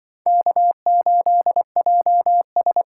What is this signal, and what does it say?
Morse code, heard as a keyed tone, about 25 wpm, 700 Hz. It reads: K8JS
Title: Speed Racing - USA 1x2 Call Signs 20wpm